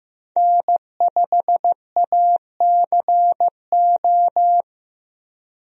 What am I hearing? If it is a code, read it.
Morse code, heard as a keyed tone, about 15 wpm, 700 Hz. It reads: N5ACO
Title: USA Call Signs - 5 Characters - Repeated 3x - 15wpm